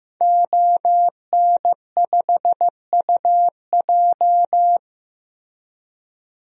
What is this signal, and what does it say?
Morse code, heard as a keyed tone, about 16 wpm, 700 Hz. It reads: ON5UJ
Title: DXCC-100 Call Signs - 5 Characters - Repeated 2x - 15wpm